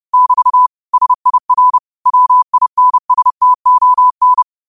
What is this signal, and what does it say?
Morse code, heard as a keyed tone, about 30 wpm, 1000 Hz. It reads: <BT> SIR WINSTON
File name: qotdfiles / 30wpm